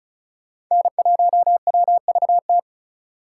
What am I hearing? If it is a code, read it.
Morse code, heard as a keyed tone, about 44 wpm, 700 Hz. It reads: N1WVT
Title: Speed Racing - USA 1x3 Call Signs 35wpm